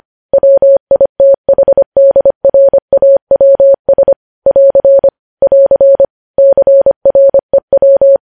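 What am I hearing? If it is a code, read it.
Morse code, heard as a keyed tone, about 25 wpm, 550 Hz. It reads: WITHDRAWS <AR> <AR> CREW
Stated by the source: Morse code news headlines at 25 words per minute.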